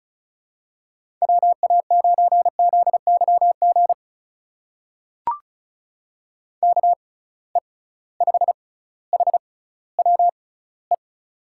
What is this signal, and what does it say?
Morse code, heard as a keyed tone, about 36 wpm, spaced as for 18 wpm, 700 Hz. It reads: WA9ZYG KE5HWE
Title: USA Call Signs - 6 Characters - 6x Character Spacing - 35wpm